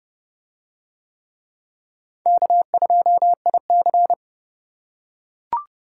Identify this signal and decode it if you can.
Morse code, heard as a keyed tone, about 26 wpm, 700 Hz. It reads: K2IC